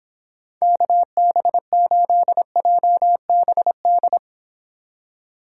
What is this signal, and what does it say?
Morse code, heard as a keyed tone, about 26 wpm, 700 Hz. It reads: KB8JBD